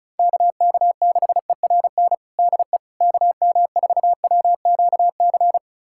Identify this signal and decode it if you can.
Morse code, heard as a keyed tone, about 34 wpm, 700 Hz. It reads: KK6ERN DE KM4WQC
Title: USA Callsign DE Callsign - 6 Characters - 35wpm